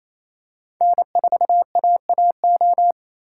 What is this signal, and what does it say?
Morse code, heard as a keyed tone, about 28 wpm, 700 Hz. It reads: N4AAO
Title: USA Call Signs - 2x Character Spacing - 28wpm